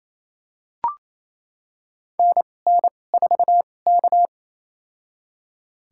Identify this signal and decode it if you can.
Morse code, heard as a keyed tone, about 29 wpm, spaced as for 21 wpm, 700 Hz. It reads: NN4K
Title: USA Call Signs - 2x Character Spacing - 28wpm